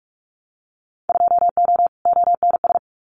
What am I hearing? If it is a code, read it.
Morse code, heard as a keyed tone, about 46 wpm, 700 Hz. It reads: 2O ONS
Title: Winter Field Day Exchanges Encore 45wpm